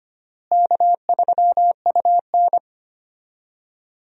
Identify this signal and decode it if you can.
Morse code, heard as a keyed tone, about 25 wpm, 700 Hz. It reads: K3UN
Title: USA Call Signs - 4 Characters - Repeated 2x - 25wpm